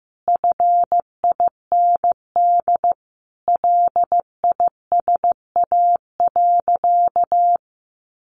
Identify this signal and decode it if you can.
Morse code, heard as a keyed tone, about 14 wpm, 700 Hz. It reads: FIND LISA.
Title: Additional Easy Phrases 15wpm